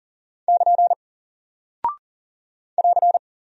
Morse code, heard as a keyed tone, about 39 wpm, 700 Hz. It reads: <KN> <AR>
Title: Single Character (Letter:Number:Punctuation:Prosign) 40wpm